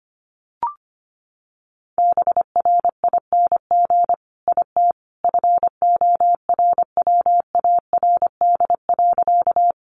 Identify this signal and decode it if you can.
Morse code, heard as a keyed tone, about 25 wpm, 700 Hz. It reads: BRING IT FORWARD.
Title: Sentences from Top 1300 Words 25wpm